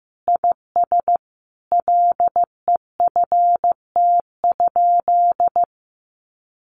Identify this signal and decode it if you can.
Morse code, heard as a keyed tone, about 15 wpm, 700 Hz. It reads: IS LEFT?
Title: Sentences from Top 1300 Words 15wpm